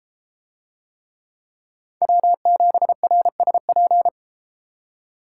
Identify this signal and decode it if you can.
Morse code, heard as a keyed tone, about 32 wpm, 700 Hz. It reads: W7RSP